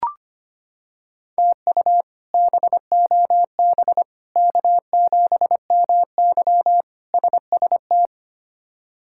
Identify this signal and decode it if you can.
Morse code, heard as a keyed tone, about 25 wpm, 700 Hz. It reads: TU BOB K7MY SST
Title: K1USN SST QSO Practice - 25wpm